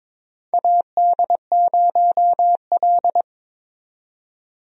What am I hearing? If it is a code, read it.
Morse code, heard as a keyed tone, about 22 wpm, 700 Hz. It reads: AD0L
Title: USA Call Signs - 4 Characters - 22wpm